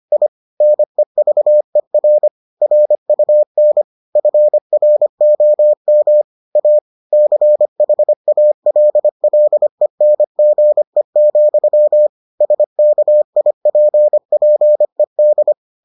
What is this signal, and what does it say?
Morse code, heard as a keyed tone, about 25 wpm, 600 Hz. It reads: I NEVER RUN FROM A CHALLENGE, SKIPPED